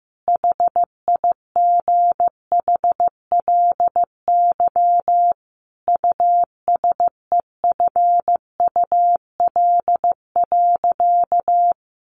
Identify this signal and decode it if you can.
Morse code, heard as a keyed tone, about 15 wpm, 700 Hz. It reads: HIGHLY USEFUL.
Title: Sentences from Top 1100 Words 15wpm